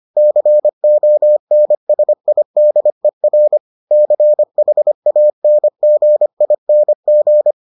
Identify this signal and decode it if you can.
Morse code, heard as a keyed tone, about 25 wpm, 600 Hz. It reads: CONSIDER CHANGING